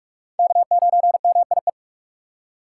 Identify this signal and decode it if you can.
Morse code, heard as a keyed tone, about 46 wpm, 700 Hz. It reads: K9MIE
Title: USA Call Signs - 5 Characters - 4x Character Spacing - 45wpm